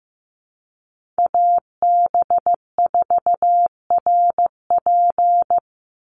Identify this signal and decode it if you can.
Morse code, heard as a keyed tone, about 15 wpm, 700 Hz. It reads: AB4RP
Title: USA Call Signs 15wpm